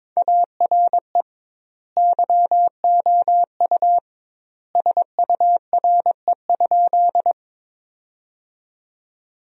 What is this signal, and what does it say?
Morse code, heard as a keyed tone, about 22 wpm, 700 Hz. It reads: ARE YOU SURE?
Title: 2x Word-Spacing - Sentences from Top 300 Words - 22wpm